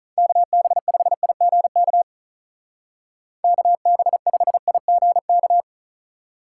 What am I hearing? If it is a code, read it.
Morse code, heard as a keyed tone, about 43 wpm, 700 Hz. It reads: KB5IGK KB5IGK
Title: Speed Racing - USA 2x3 Call Signs 35wpm